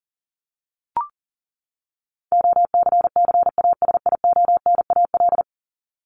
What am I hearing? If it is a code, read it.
Morse code, heard as a keyed tone, about 37 wpm, 700 Hz. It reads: OCCASIONAL
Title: LY Suffix - Top 100 Words with Base Words 40wpm